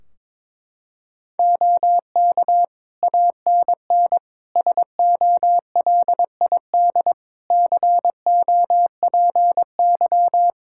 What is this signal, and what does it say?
Morse code, heard as a keyed tone, about 22 wpm, 700 Hz. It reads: OK ANN SOLID COPY
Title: K1USN SST QSO Practice 22wpm